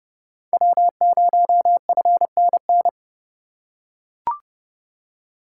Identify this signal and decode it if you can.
Morse code, heard as a keyed tone, about 32 wpm, 700 Hz. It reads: W0FNN
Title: USA Call Signs - 5 Characters - 2x Character Spacing - 30wpm